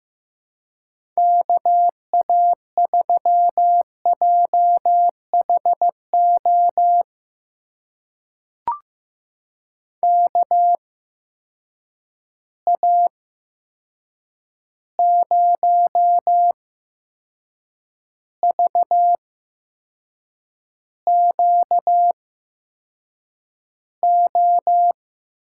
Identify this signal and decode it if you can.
Morse code, heard as a keyed tone, about 15 wpm, spaced as for 6 wpm, 700 Hz. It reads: KA3JHO KA0VQO
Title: USA Call Signs - 6 Characters - 8x Character Spacing - 15wpm